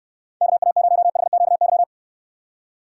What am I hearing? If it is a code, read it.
Morse code, heard as a keyed tone, about 66 wpm, 700 Hz. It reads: KA0HQY